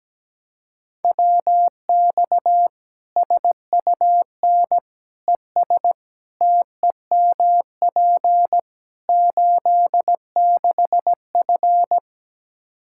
Morse code, heard as a keyed tone, about 17 wpm, 700 Hz. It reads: WX SUN ES TEMP 86F
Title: K1USN SST QSO Practice 17wpm